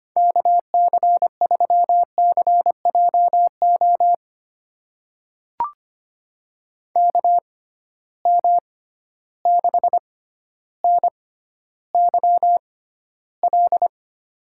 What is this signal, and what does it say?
Morse code, heard as a keyed tone, about 25 wpm, spaced as for 13 wpm, 700 Hz. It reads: KC3CJO KM6NYL
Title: USA Call Signs - 6 Characters - 6x Character Spacing - 25wpm